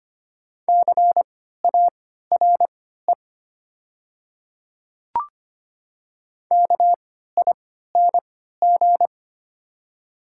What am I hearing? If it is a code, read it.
Morse code, heard as a keyed tone, about 25 wpm, spaced as for 14 wpm, 690 Hz. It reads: CARE KING
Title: 3x Character Spacing - Top 500 Words - 25wpm